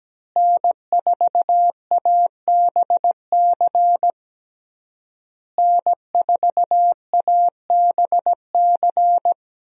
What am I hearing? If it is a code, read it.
Morse code, heard as a keyed tone, about 17 wpm, 700 Hz. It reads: N4ABC N4ABC
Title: USA Call Signs - 5 Characters - Repeated 3x - 17wpm